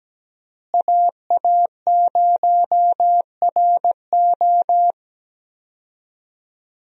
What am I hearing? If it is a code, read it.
Morse code, heard as a keyed tone, about 17 wpm, 700 Hz. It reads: AA0RO
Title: USA Call Signs - 5 Characters - Repeated 3x - 17wpm